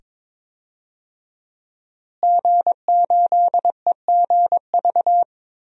Morse code, heard as a keyed tone, about 22 wpm, 700 Hz. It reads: G8EGV
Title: DXCC-100 Call Signs - 5 Characters - Repeated 2x - 22wpm